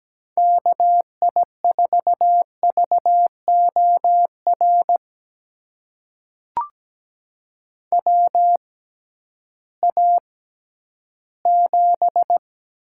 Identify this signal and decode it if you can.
Morse code, heard as a keyed tone, about 17 wpm, spaced as for 11 wpm, 700 Hz. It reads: KI4VOR WA7
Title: USA Call Signs - 6 Characters - 6x Character Spacing - 17wpm